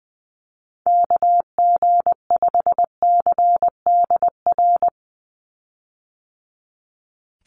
Morse code, heard as a keyed tone, about 20 wpm, 700 Hz. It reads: KG5CDR
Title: USA Call Signs 20wpm